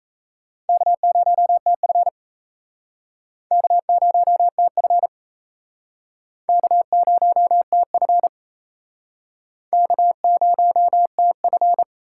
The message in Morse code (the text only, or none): K0TF K0TF K0TF K0TF